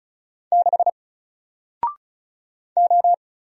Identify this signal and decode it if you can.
Morse code, heard as a keyed tone, about 36 wpm, 700 Hz. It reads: 6 O
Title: Single Character (Letter:Number:Punctuation:Prosign) Alt 35wpm